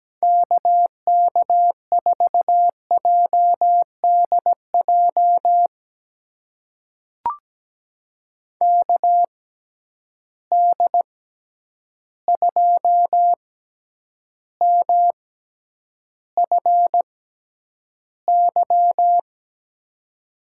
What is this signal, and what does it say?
Morse code, heard as a keyed tone, about 17 wpm, spaced as for 9 wpm, 700 Hz. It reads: KK4JDJ KD2MFY
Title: USA Call Signs - 6 Characters - 6x Character Spacing - 17wpm